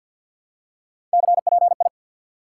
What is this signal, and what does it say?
Morse code, heard as a keyed tone, about 48 wpm, 700 Hz. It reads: CPI
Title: Top 90 QSO Elements 50wpm